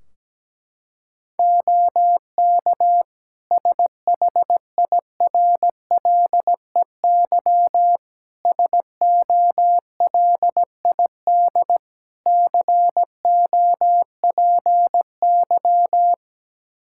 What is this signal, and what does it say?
Morse code, heard as a keyed tone, about 17 wpm, 700 Hz. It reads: OK SHIRLEY SOLID COPY
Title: Practice QSO - Exchange 3 - More Info - 17wpm